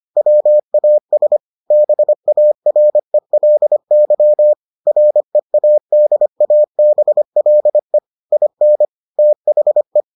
Morse code, heard as a keyed tone, about 25 wpm, 600 Hz. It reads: WAS BARELY READABLE IN THE